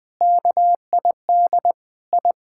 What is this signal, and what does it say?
Morse code, heard as a keyed tone, about 21 wpm, 700 Hz. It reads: KID I
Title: Sets of 6 Words - 20wpm